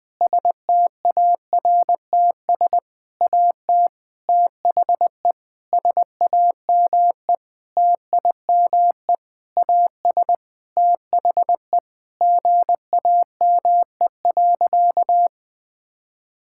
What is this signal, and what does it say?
Morse code, heard as a keyed tone, about 20 wpm, 700 Hz. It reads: STARTS AT THE SAME TIME AS THE GAME.
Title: Sentences with 6-Word Phrases - Spoken Beginning - 20wpm